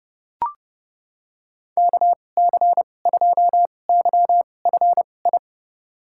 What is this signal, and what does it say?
Morse code, heard as a keyed tone, about 30 wpm, spaced as for 22 wpm, 700 Hz. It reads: KC2YFI
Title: USA Call Signs - 6 Characters - 2x Character Spacing - 30wpm